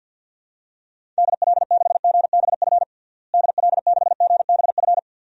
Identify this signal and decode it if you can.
Morse code, heard as a keyed tone, about 52 wpm, 700 Hz. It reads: DL6GBF DL6GBF
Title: DXCC-100 Call Signs - 6 Characters - Repeated 2x - 50wpm